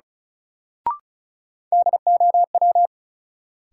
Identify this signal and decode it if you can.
Morse code, heard as a keyed tone, about 36 wpm, 700 Hz. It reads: DOW
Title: Sets of 5 Words - New Word Drill - 35wpm